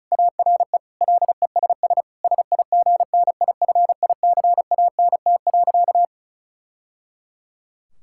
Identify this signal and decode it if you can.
Morse code, heard as a keyed tone, about 36 wpm, 700 Hz. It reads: ARE LESS SIGNIFICANT.
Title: Sentences from Top 1000 Words - Review Part 6 - 35wpm